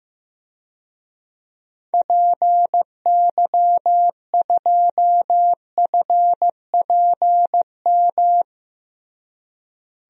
Morse code, heard as a keyed tone, about 15 wpm, 700 Hz. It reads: PY2FPM
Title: DXCC-100 Call Signs - 6 Characters - Repeated 2x - 15wpm